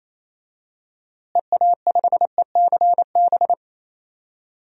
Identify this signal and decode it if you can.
Morse code, heard as a keyed tone, about 28 wpm, 700 Hz. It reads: EA5ECB
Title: DXCC-100 Call Signs - 6 Characters - Repeated 3x - 28wpm